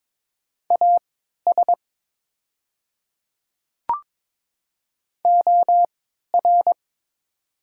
Morse code, heard as a keyed tone, about 22 wpm, spaced as for 12 wpm, 690 Hz. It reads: AS OR
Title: 3x Character Spacing - Top 100 Words - 22wpm